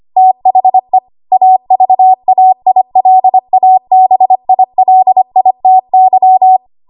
Morse code, heard as a keyed tone, about 25 wpm, 750 Hz. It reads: THE AVAILABILITY
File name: Archive / 25 WPM / 25WPM